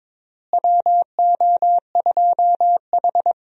Morse code, heard as a keyed tone, about 22 wpm, 700 Hz. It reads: WO2H
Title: USA Call Signs - 4 Characters - Repeated 3x - 22wpm